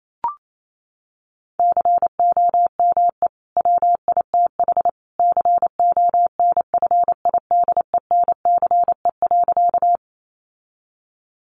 Morse code, heard as a keyed tone, about 28 wpm, 700 Hz. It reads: COME WITH CONFIDENCE.
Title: Sentences from Top 1500 Words 28wpm